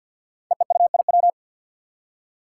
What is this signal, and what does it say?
Morse code, heard as a keyed tone, about 52 wpm, 700 Hz. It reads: EEUIW
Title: Five Challenging Letter-Number Combinations 50wpm